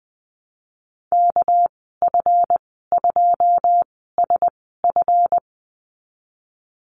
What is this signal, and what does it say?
Morse code, heard as a keyed tone, about 20 wpm, spaced as for 15 wpm, 700 Hz. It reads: KF2SF
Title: 207 Lesson 33g - Letter j - Call Signs 20wpm